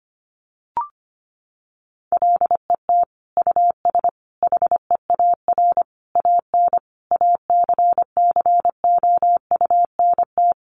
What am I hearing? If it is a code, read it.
Morse code, heard as a keyed tone, about 25 wpm, 700 Hz. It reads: LET US HEAR AN ACCOUNT
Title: Sentences from Top 900 Words 25wpm